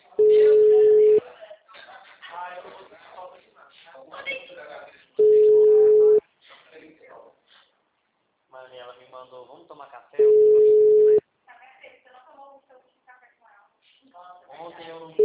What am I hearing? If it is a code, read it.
Ringback tone (Europe, 425 Hz), with speech.